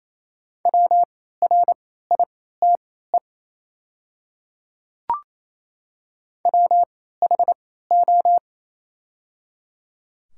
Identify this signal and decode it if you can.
Morse code, heard as a keyed tone, about 27 wpm, spaced as for 16 wpm, 690 Hz. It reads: WRITE WHO
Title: 3x Character Spacing - Top 100 Words - 28wpm